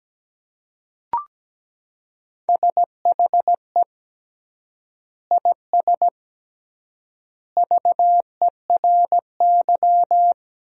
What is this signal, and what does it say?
Morse code, heard as a keyed tone, about 17 wpm, 700 Hz. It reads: SHE IS VERY
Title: 3x Word-Spacing - Sentences from Top 400 Words - 17wpm